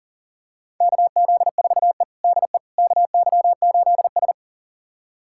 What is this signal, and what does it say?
Morse code, heard as a keyed tone, about 40 wpm, 700 Hz. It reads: KZ4E DE KY8S
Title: USA Callsign DE Callsign - 4 Characters - 40wpm